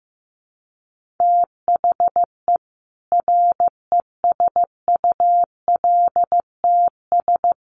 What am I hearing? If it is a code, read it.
Morse code, heard as a keyed tone, about 15 wpm, 700 Hz. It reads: THE RESULTS
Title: Sets of 5 Words 15wpm